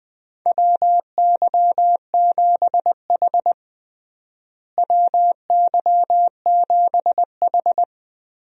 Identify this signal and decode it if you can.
Morse code, heard as a keyed tone, about 20 wpm, 700 Hz. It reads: WY7H WY7H
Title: USA Call Signs - 4 Characters - Repeated 3x - 20wpm